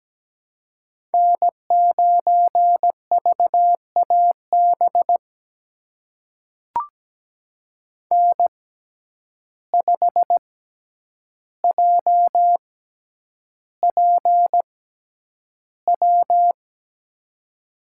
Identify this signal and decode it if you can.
Morse code, heard as a keyed tone, about 17 wpm, spaced as for 9 wpm, 700 Hz. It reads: N9VAB N5JPW
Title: USA Call Signs - 5 Characters - 6x Character Spacing - 17wpm